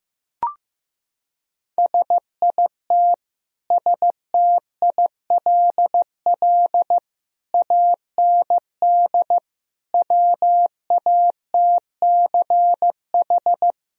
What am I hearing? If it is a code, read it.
Morse code, heard as a keyed tone, about 15 wpm, 700 Hz. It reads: SIT STILL AND WATCH
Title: Sentences from Top 1000 Words - Review Part 1 - 15wpm